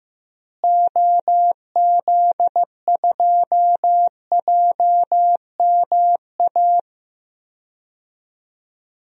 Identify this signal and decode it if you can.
Morse code, heard as a keyed tone, about 15 wpm, 700 Hz. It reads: OZ2JMA
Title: DXCC-100 Call Signs - 6 Characters - Repeated 3x - 15wpm